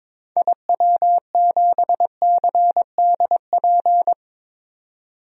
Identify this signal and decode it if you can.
Morse code, heard as a keyed tone, about 22 wpm, 700 Hz. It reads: IW7CDP
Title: DXCC-100 Call Signs - 6 Characters - 22wpm